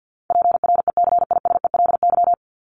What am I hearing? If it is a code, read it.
Morse code, heard as a keyed tone, about 50 wpm, 700 Hz. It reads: PRECISELY
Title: LY Suffix - Top 100 Words with Base Words 50wpm